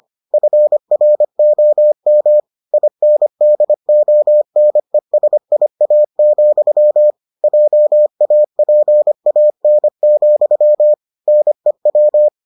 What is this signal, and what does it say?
Morse code, heard as a keyed tone, about 25 wpm, 600 Hz. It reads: FROM INDONESIA, JAPAN, NEW